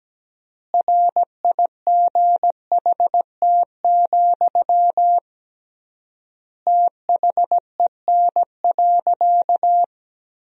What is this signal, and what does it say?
Morse code, heard as a keyed tone, about 17 wpm, 700 Hz. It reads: RIGHT, THEN.
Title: 3x Word-Spacing - Sentences from Top 200 Words - 17wpm